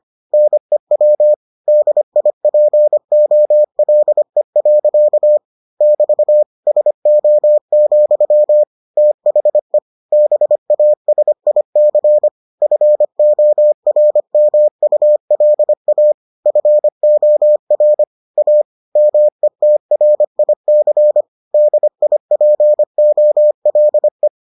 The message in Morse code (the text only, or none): NEW DIPOLE. <BT> SO, THE BASIC FORMULA FOR A METRIC DIPOLE